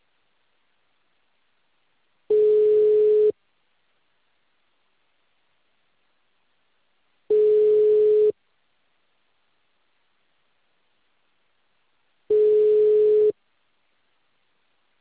Ringback tone (Europe, 425 Hz).
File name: out- -2044- - - .31221